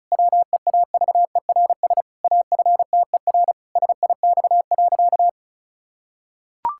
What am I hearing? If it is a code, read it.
Morse code, heard as a keyed tone, about 36 wpm, 700 Hz. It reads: WEAVERS AFTER SIX.
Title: Pangram Challenge v2 - 35wpm